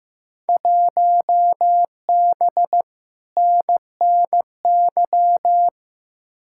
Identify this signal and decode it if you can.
Morse code, heard as a keyed tone, about 15 wpm, 700 Hz. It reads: 1B NNY
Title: Field Day Exchanges - 15wpm